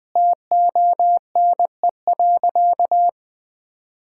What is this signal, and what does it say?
Morse code, heard as a keyed tone, about 20 wpm, 700 Hz. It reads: TONE.
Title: Sentences from Top 1600 Words 20wpm